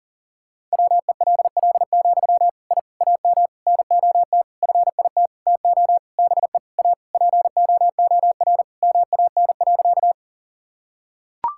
Morse code, heard as a keyed tone, about 40 wpm, 700 Hz. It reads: WELL, I AM NOT FIT TO BE A POOR MAN.
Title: Sentences from Top 1000 Words - Review Part 5 - 40wpm